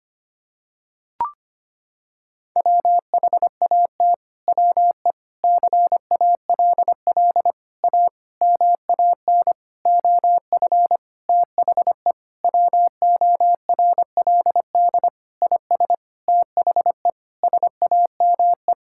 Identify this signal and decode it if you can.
Morse code, heard as a keyed tone, about 25 wpm, 700 Hz. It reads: WHAT WE CALL A MAN OF THE WORLD IS THE SAME
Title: Sentences from Top 1600 Words 25wpm